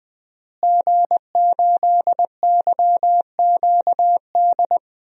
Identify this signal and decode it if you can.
Morse code, heard as a keyed tone, about 20 wpm, 700 Hz. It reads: G8YQD
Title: DXCC-100 Call Signs - 5 Characters - Repeated 2x - 20wpm